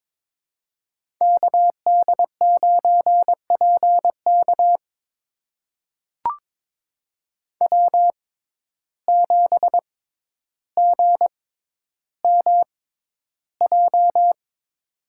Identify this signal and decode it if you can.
Morse code, heard as a keyed tone, about 22 wpm, spaced as for 11 wpm, 700 Hz. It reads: KD9PK W7GMJ
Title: USA Call Signs - 5 Characters - 6x Character Spacing - 22wpm